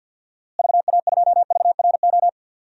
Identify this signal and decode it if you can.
Morse code, heard as a keyed tone, about 50 wpm, 700 Hz. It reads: VA2VRO
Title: DXCC-100 Call Signs - 6 Characters - Repeated 2x - 50wpm